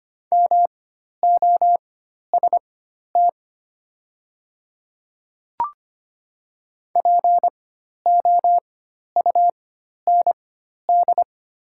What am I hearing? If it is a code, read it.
Morse code, heard as a keyed tone, about 25 wpm, spaced as for 12 wpm, 690 Hz. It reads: MOST POUND
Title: 4x Character Spacing - Top 500 Words - 25wpm